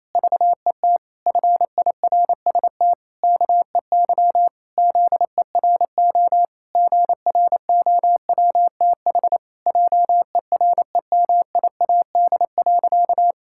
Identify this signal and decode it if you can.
Morse code, heard as a keyed tone, about 28 wpm, 700 Hz. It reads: VET FIRST KEY ZERO GROWTH JEREMIAD.
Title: Pangram Challenge v2 - 28wpm